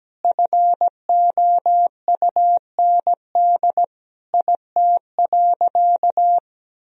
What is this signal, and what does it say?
Morse code, heard as a keyed tone, about 17 wpm, 700 Hz. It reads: FOUND IT.
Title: Sentences from Top 1000 Words - Review Part 1 - 17wpm